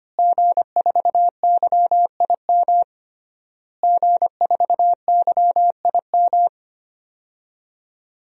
Morse code, heard as a keyed tone, about 25 wpm, 700 Hz. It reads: G4YIM G4YIM
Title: DXCC-100 Call Signs - 5 Characters - Repeated 3x - 25wpm